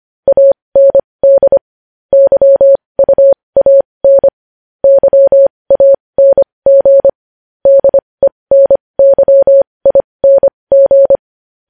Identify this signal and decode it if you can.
Morse code, heard as a keyed tone, about 25 wpm, spaced as for 20 wpm, 550 Hz. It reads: AND YUAN YANG DENYING
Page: News Headlines in Morse Code at 20 WPM